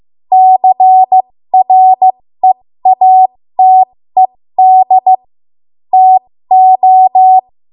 Morse code, heard as a keyed tone, about 15 wpm, spaced as for 13 wpm, 750 Hz. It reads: CREATED TO
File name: Morse / 13WPM